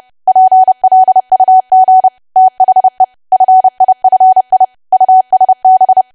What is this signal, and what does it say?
Morse code, heard as a keyed tone, about 30 wpm, 750 Hz. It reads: PLUG THE FIFI USB
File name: cwa / W1AW-Level3 / 30WPM